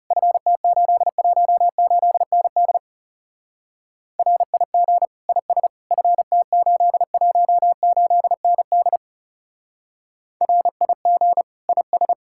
Morse code, heard as a keyed tone, about 36 wpm, 700 Hz. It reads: FT818ND RIG IS FT818ND RIG IS
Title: Speed-Racing Common QSO Phrases Encore - 30wpm